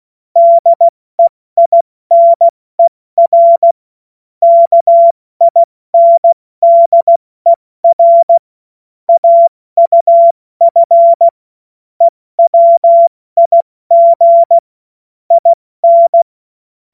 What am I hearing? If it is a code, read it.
Morse code, heard as a keyed tone, about 16 wpm, 680 Hz. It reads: DEINER KINDER AUF EWIG IN
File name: Morsekurs / Witze / Witz 09. August 2022 70 F80